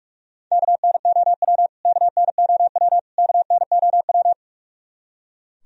Morse code, heard as a keyed tone, about 43 wpm, 700 Hz. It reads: KNOW KNOW KNOW
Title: Top 100 Words Repeated 45wpm